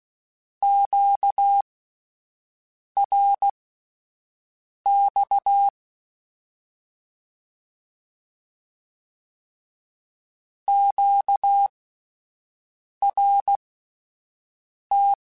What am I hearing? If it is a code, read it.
Morse code, heard as a keyed tone, about 16 wpm, spaced as for 5 wpm, 800 Hz. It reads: QRX QRT
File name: (spaced) / Q-Calls Spaced-mp3s / Q Calls Twice sp @10